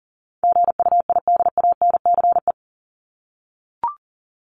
Morse code, heard as a keyed tone, about 40 wpm, 700 Hz. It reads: GUIDANCE
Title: Top 1501-1750 Words 40wpm